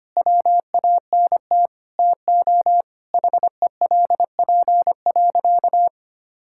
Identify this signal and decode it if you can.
Morse code, heard as a keyed tone, about 25 wpm, 700 Hz. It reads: WANT TO HELP.
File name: practice / 25wpm / Sentences with 3-Word Phrases - 25wpm